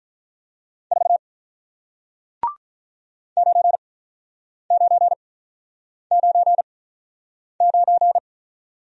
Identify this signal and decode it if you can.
Morse code, heard as a keyed tone, about 47 wpm, spaced as for 9 wpm, 690 Hz. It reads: 4 9999